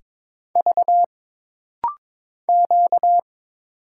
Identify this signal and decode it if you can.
Morse code, heard as a keyed tone, about 22 wpm, 700 Hz. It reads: V Q
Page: This rapid-fire timing is geared to push your ICR (Instant Character Recognition) skills.